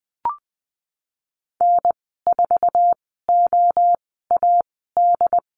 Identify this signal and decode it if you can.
Morse code, heard as a keyed tone, about 20 wpm, spaced as for 15 wpm, 700 Hz. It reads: N4OAD
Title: 088 Lesson 15g - Number 5 - Call Signs 20wpm